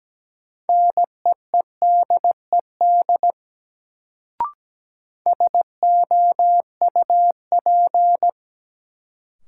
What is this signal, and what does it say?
Morse code, heard as a keyed tone, about 17 wpm, 700 Hz. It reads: NEEDED SOUP
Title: Sentences from Top 1000 Words - Review Part 6 - New Word Drill - 17wpm